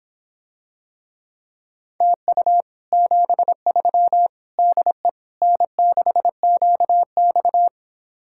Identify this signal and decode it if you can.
Morse code, heard as a keyed tone, about 26 wpm, 700 Hz. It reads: TU 73 DE N6QX